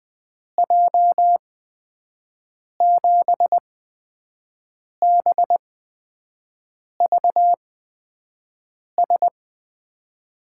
Morse code, heard as a keyed tone, about 20 wpm, spaced as for 5 wpm, 700 Hz. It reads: J7BVS
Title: USA Call Signs - 6 Characters - 8x Character Spacing - 20wpm